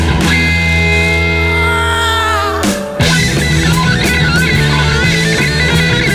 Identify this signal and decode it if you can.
Dial-up modem handshake.